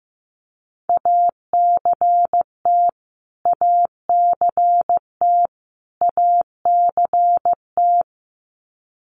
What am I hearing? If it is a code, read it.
Morse code, heard as a keyed tone, about 15 wpm, 700 Hz. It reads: ACT ACT ACT
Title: Top 500 Words Repeated 15wpm